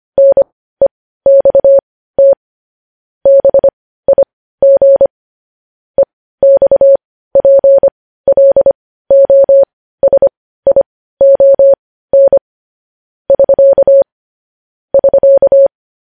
Morse code, heard as a keyed tone, about 25 wpm, spaced as for 15 wpm, 550 Hz. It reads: NEXT BIG EXPLOSION <SK> <SK>
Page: news headlines in morse code at 15 wpm